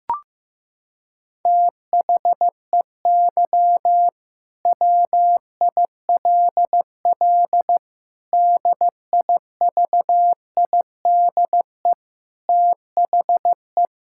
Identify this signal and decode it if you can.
Morse code, heard as a keyed tone, about 15 wpm, 700 Hz. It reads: THEY WILL DIVIDE THE
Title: Sentences from Top 1600 Words 15wpm